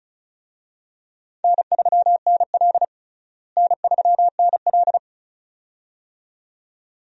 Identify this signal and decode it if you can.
Morse code, heard as a keyed tone, about 34 wpm, 700 Hz. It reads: N3NL N3NL
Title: USA Call Signs - 4 Characters - Repeated 2x - 35wpm